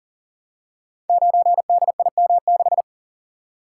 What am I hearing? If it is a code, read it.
Morse code, heard as a keyed tone, about 40 wpm, 700 Hz. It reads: 9DIM6
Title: Code Groups - 5 Letter:Numbers - 40wpm